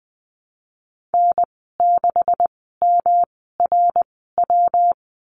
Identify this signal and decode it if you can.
Morse code, heard as a keyed tone, about 20 wpm, spaced as for 15 wpm, 700 Hz. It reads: N6MRW
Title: 179 Lesson 29g - Slash - Call Signs 20wpm